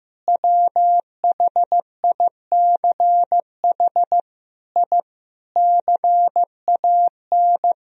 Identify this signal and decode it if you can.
Morse code, heard as a keyed tone, about 15 wpm, 700 Hz. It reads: WHICH I CAN
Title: Sentences from Top 1000 Words - Review Part 2 - 15wpm